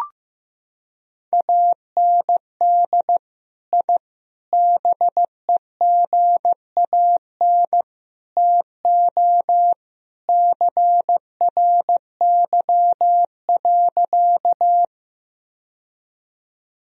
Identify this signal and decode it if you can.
Morse code, heard as a keyed tone, about 15 wpm, 700 Hz. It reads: AND I BEGAN TO CRY.
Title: Sentences from Top 1000 Words - Review Part 2 - 15wpm